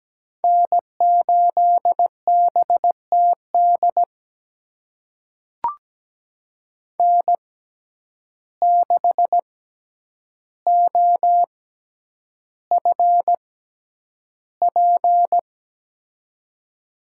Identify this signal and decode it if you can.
Morse code, heard as a keyed tone, about 17 wpm, spaced as for 9 wpm, 700 Hz. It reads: N8BTD N6OFP
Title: USA Call Signs - 5 Characters - 6x Character Spacing - 17wpm